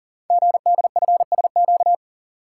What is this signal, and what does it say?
Morse code, heard as a keyed tone, about 40 wpm, 700 Hz. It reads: GDFSQ
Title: Code Groups - 5 Letters - 40wpm